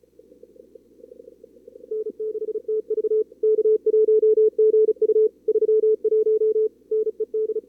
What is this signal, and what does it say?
Morse code, heard as a keyed tone, about 33 wpm, 440 Hz. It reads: N6TV K1GU 31 NED
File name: CWSprint / Feb2022 / k5tr-feb-na-sprint-cw-k5ot -2300